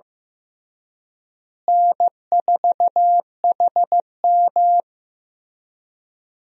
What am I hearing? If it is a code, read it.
Morse code, heard as a keyed tone, about 16 wpm, 700 Hz. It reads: N4HM
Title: USA Call Signs - 4 Characters - Repeated 3x - 15wpm